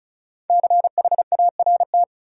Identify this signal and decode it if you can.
Morse code, heard as a keyed tone, about 36 wpm, 700 Hz. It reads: CHART
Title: Sentences from Top 1300 Words - New Word Drill Alt - 35wpm